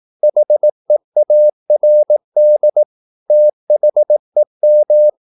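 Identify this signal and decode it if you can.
Morse code, heard as a keyed tone, about 18 wpm, 600 Hz. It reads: HEARD THEM
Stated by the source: Morse practice at 18 wpm, English text, from CW med Gnister.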